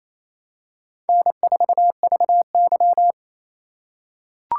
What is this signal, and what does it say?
Morse code, heard as a keyed tone, about 28 wpm, 700 Hz. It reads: N4VY
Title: USA Call Signs - 4 Characters - 8x Character Spacing - 28wpm